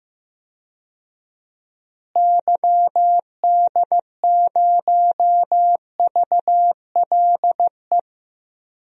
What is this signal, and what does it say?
Morse code, heard as a keyed tone, about 15 wpm, 700 Hz. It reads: YD0VLE
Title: DXCC-100 Call Signs - 6 Characters - Repeated 3x - 15wpm